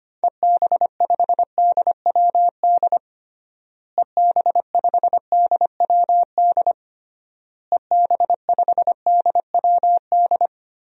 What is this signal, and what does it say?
Morse code, heard as a keyed tone, about 25 wpm, 700 Hz. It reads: EB5DWD EB5DWD EB5DWD
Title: DXCC-100 Call Signs - 6 Characters - Repeated 3x - 25wpm